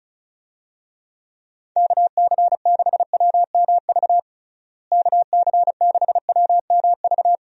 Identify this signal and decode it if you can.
Morse code, heard as a keyed tone, about 36 wpm, 700 Hz. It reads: KC6WMV KC6WMV
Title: USA Call Signs - 6 Characters - Repeated 3x - 35wpm